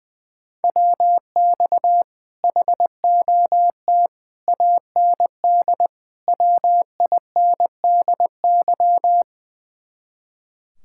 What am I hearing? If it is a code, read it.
Morse code, heard as a keyed tone, about 20 wpm, 700 Hz. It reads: WX HOT AND WINDY
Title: Speed-Racing Common QSO Phrases - 20wpm